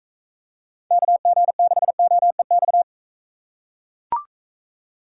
Morse code, heard as a keyed tone, about 41 wpm, 700 Hz. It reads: KG6OEX